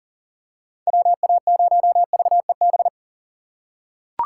Morse code, heard as a keyed tone, about 40 wpm, 700 Hz. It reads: WA0VEB
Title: USA Call Signs - 3x Character Spacing - 40wpm